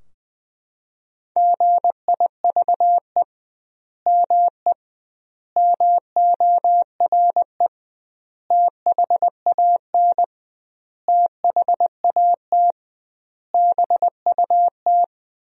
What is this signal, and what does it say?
Morse code, heard as a keyed tone, about 20 wpm, 700 Hz. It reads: GIVE ME MORE THAN THAT BUT
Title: 2x Word-Spacing - Sentences from Top 200 Words - 20wpm